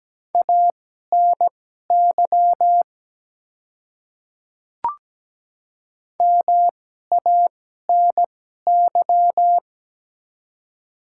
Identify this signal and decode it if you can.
Morse code, heard as a keyed tone, about 17 wpm, spaced as for 12 wpm, 700 Hz. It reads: ANY MANY
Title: 2x Character Spacing - Top 200 Words - 17wpm